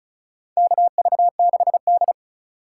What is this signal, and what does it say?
Morse code, heard as a keyed tone, about 36 wpm, 700 Hz. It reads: KV6D
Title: USA Call Signs - 3x Character Spacing - 35wpm